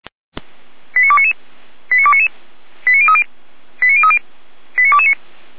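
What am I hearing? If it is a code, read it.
Five-tone selective call (ZVEI), heard as quick runs of five tones, tones of about 70 ms. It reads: 89299 89299 89030 89030 89229
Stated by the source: ZVEI five-tone call 89173 received on an SDR, from the Feuerwehr Orlamünde fire brigade server.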